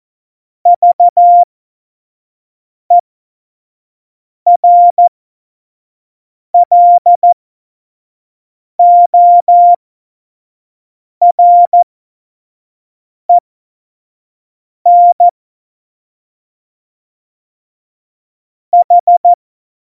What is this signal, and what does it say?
Morse code, heard as a keyed tone, about 14 wpm, spaced as for 5 wpm, 700 Hz. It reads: VERLOREN H
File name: Morsekurs / Witze / Witz 09. April 2019, 25 F70